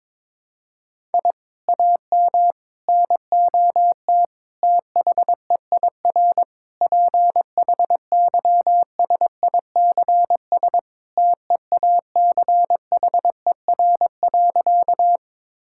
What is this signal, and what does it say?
Morse code, heard as a keyed tone, about 22 wpm, 700 Hz. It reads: I AM NOT THEIR PHYSICS TEACHER.
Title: Sentences from Top 1000 Words - Review Part 6 - 22wpm